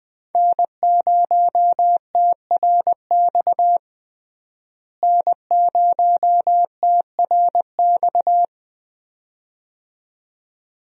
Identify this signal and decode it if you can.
Morse code, heard as a keyed tone, about 20 wpm, 700 Hz. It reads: N0TRX N0TRX
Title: USA Call Signs - 5 Characters - Repeated 2x - 20wpm